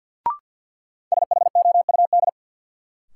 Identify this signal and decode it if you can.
Morse code, heard as a keyed tone, about 51 wpm, 700 Hz. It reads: SHOUD
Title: Sentences from Top 1000 Words - Review Part 4 - New Word Drill - 50wpm